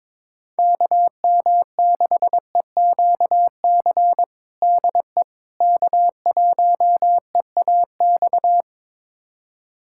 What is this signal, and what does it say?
Morse code, heard as a keyed tone, about 22 wpm, 700 Hz. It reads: KM6EQC DE K1EAX
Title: USA Callsign DE Callsign 22wpm